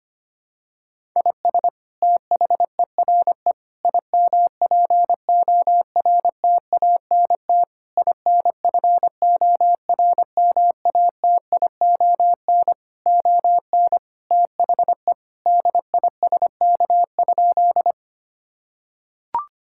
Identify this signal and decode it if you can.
Morse code, heard as a keyed tone, about 25 wpm, 700 Hz. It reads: IS THERE IMPORTANT INFORMATION ON THE DISK?
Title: Sentences from Top 1000 Words - Review Part 6 - 25wpm